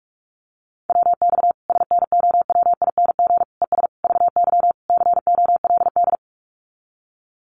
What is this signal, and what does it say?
Morse code, heard as a keyed tone, about 44 wpm, 700 Hz. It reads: WX SNOWING ES VY COLD
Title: Common QSO Phrases 45wpm